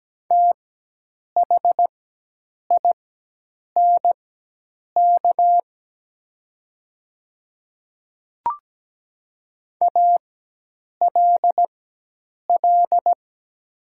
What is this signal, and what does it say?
Morse code, heard as a keyed tone, about 17 wpm, spaced as for 8 wpm, 700 Hz. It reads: THINK ALL
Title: 4x Character Spacing - Top 300 Words - 17wpm